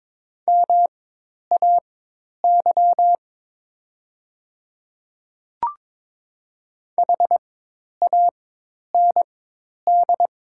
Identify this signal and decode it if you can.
Morse code, heard as a keyed tone, about 22 wpm, spaced as for 10 wpm, 700 Hz. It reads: MAY HAND
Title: 4x Character Spacing - Top 200 Words - 22wpm